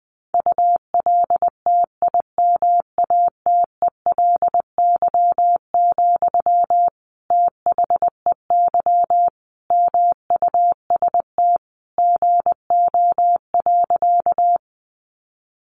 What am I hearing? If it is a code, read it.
Morse code, heard as a keyed tone, about 20 wpm, 700 Hz. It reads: ULTIMATELY, THEY MUST GO.
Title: Sentences from Top 1200 Words 20wpm